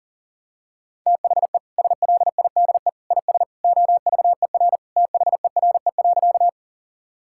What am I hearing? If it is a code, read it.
Morse code, heard as a keyed tone, about 40 wpm, 700 Hz. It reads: THE SLIDE IS OVER THERE.
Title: Sentences from Top 1600 Words 40wpm